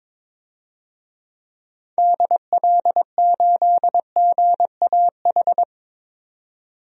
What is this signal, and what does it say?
Morse code, heard as a keyed tone, about 22 wpm, 700 Hz. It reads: DL8GAH
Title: DXCC-100 Call Signs - 6 Characters - 22wpm